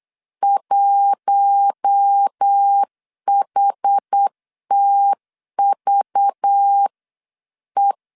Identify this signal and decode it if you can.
Morse code, heard as a keyed tone, about 8 wpm, 800 Hz. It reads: 1HTVE
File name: LEZIONI / 6 PKX0 / 6 PKX0 8wpm 3341